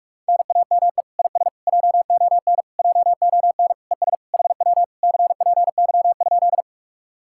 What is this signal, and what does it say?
Morse code, heard as a keyed tone, about 50 wpm, 700 Hz. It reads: NAME IS JON JON ES HW CPY?